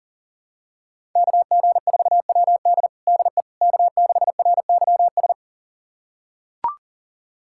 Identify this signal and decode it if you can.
Morse code, heard as a keyed tone, about 40 wpm, 700 Hz. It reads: KG4WD DE K6RYS
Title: USA Callsign DE Callsign - 5 Characters - 40wpm